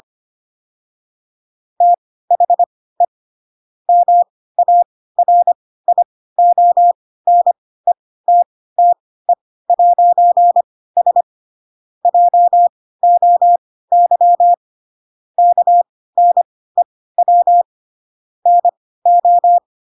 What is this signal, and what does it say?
Morse code, heard as a keyed tone, about 25 wpm, spaced as for 16 wpm, 700 Hz. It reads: THE MARIONETTE'S JOY KNEW NO
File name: Pinocchio / Pinocchio ch5 10 25 ik1hge